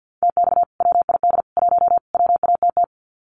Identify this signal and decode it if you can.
Morse code, heard as a keyed tone, about 50 wpm, 700 Hz. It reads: TX WID 1 WATT
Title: Common QSO Phrases 50wpm